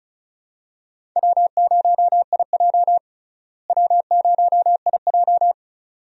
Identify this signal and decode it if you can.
Morse code, heard as a keyed tone, about 36 wpm, 700 Hz. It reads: W0IJ W0IJ
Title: USA Call Signs - 4 Characters - Repeated 2x - 35wpm